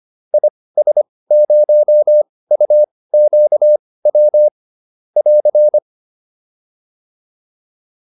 Morse code, heard as a keyed tone, about 25 wpm, spaced as for 18 wpm, 600 Hz. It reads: IS0UQW <AR>